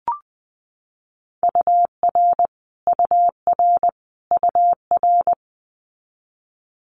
Morse code, heard as a keyed tone, about 20 wpm, 700 Hz. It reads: UR UR UR
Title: Abbreviations Repeated 3x - 20wpm